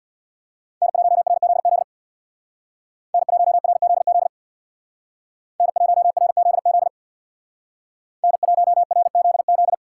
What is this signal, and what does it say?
Morse code, heard as a keyed tone, about 63 wpm, 700 Hz. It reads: N1RZZ N1RZZ N1RZZ N1RZZ